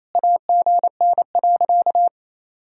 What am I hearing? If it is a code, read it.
Morse code, heard as a keyed tone, about 28 wpm, 700 Hz. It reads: AGN.
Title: Sentences from Top 300 Words and Abbreviations 28wpm